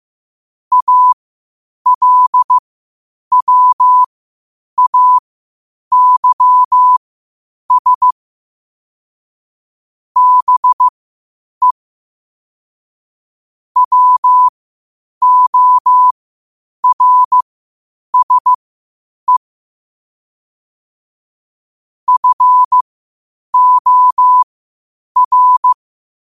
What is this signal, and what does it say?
Morse code, heard as a keyed tone, about 15 wpm, spaced as for 8 wpm, 1000 Hz. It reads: ALWAYS BE WORSE FOR
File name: qotdfiles / 8wpm